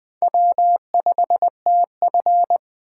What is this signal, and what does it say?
Morse code, heard as a keyed tone, about 20 wpm, 700 Hz. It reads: W5TF
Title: Speed Racing - USA Call Signs 17wpm